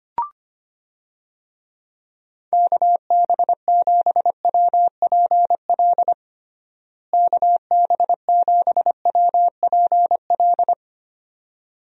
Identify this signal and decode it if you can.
Morse code, heard as a keyed tone, about 25 wpm, 700 Hz. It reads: KB7WPL KB7WPL
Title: USA Call Signs - 6 Characters - Repeated 2x - 25wpm